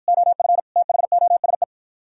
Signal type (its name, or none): Morse code